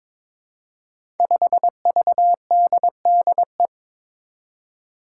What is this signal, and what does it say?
Morse code, heard as a keyed tone, about 22 wpm, 700 Hz. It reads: 5VDDE
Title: Five Challenging Letter-Number Combinations 22wpm